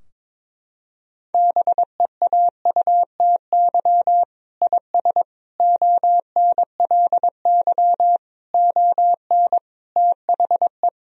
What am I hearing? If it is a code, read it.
Morse code, heard as a keyed tone, about 22 wpm, 700 Hz. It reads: BEAUTY IS ONLY ON THE
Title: Sentences from Top 1000 Words - Review Part 3 - 22wpm